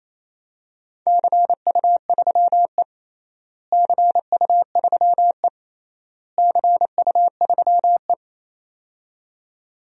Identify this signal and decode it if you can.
Morse code, heard as a keyed tone, about 28 wpm, 700 Hz. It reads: CU3E CU3E CU3E
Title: DXCC-100 Call Signs - 4 Characters - Repeated 3x - 28wpm